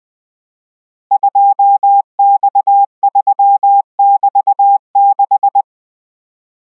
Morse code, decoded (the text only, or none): 2X3<BT>6